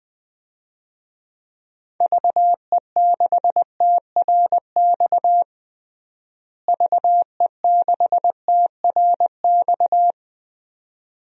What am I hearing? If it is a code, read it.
Morse code, heard as a keyed tone, about 20 wpm, 700 Hz. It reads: VE6TRX VE6TRX
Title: DXCC-100 Call Signs - 6 Characters - Repeated 2x - 20wpm